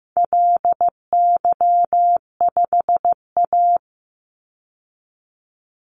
Morse code, heard as a keyed tone, about 15 wpm, 700 Hz. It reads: LY5A
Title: Call Signs Repeated 2x - 15wpm